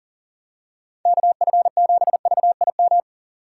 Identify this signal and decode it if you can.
Morse code, heard as a keyed tone, about 38 wpm, 700 Hz. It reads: KF7VIM
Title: USA Call Signs - 6 Characters - 2x Character Spacing - 40wpm